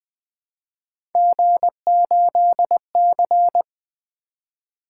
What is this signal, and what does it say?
Morse code, heard as a keyed tone, about 20 wpm, 700 Hz. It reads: G8C
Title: Three Letter-Number Combinations - 20wpm